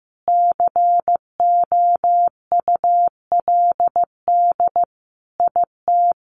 Morse code, heard as a keyed tone, about 15 wpm, 700 Hz. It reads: COULD IT
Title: Sentences from Top 1300 Words 15wpm